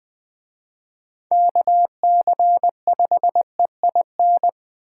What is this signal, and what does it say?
Morse code, heard as a keyed tone, about 24 wpm, 700 Hz. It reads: KC5EIN
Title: Speed Racing - USA 2x3 Call Signs 20wpm